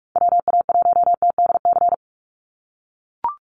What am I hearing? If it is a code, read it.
Morse code, heard as a keyed tone, about 44 wpm, 700 Hz. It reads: WA1TDC
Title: USA Call Signs 45wpm